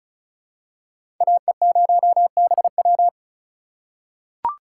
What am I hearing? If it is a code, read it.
Morse code, heard as a keyed tone, about 34 wpm, 700 Hz. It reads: AE0BW
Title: USA Call Signs - 3x Character Spacing - 35wpm